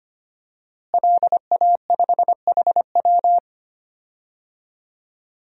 Morse code, heard as a keyed tone, about 25 wpm, 700 Hz. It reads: LA5HW
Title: DXCC-100 Call Signs - 5 Characters - Repeated 3x - 25wpm